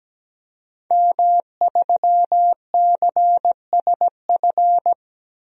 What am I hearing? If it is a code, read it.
Morse code, heard as a keyed tone, about 17 wpm, 700 Hz. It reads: M3CSF
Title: DXCC-100 Call Signs - 5 Characters - 17wpm